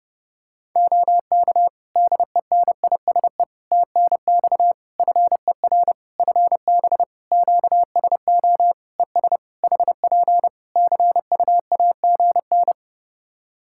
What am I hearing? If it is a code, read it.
Morse code, heard as a keyed tone, about 30 wpm, 700 Hz. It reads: OK DENISE TNX FER FB QSO ES HP CUAGN
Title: Practice QSO - Exchange 4 - Ending - 30wpm